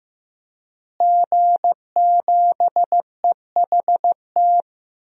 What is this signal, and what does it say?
Morse code, heard as a keyed tone, about 15 wpm, 700 Hz. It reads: G7EHT
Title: DXCC-100 Call Signs - 5 Characters - Repeated 2x - 15wpm